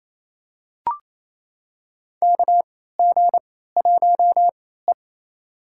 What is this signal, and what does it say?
Morse code, heard as a keyed tone, about 28 wpm, spaced as for 16 wpm, 700 Hz. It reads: KG1E
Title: USA Call Signs - 3x Character Spacing - 28wpm